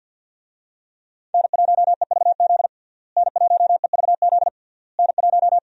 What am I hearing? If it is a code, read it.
Morse code, heard as a keyed tone, about 51 wpm, 700 Hz. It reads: N1EVZ N1EVZ N1
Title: USA Call Signs - 5 Characters - Repeated 3x - 50wpm